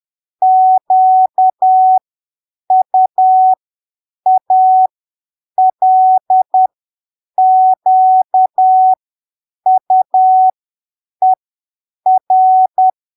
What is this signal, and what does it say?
Morse code, heard as a keyed tone, about 10 wpm, spaced as for 7 wpm, 750 Hz. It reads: QUALQUER
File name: arquivos / texto / texto 09 10ppm